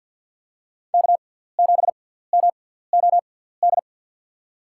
Morse code, heard as a keyed tone, about 51 wpm, spaced as for 17 wpm, 700 Hz. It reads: K7MOD
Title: USA Call Signs - 6x Character Spacing - 50wpm